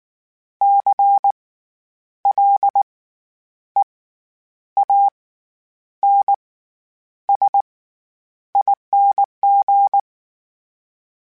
Morse code, heard as a keyed tone, about 19 wpm, spaced as for 9 wpm, 800 Hz. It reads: CLEANSING
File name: Special INGs / Special INGs-mp3s / SlowToFaster ing @18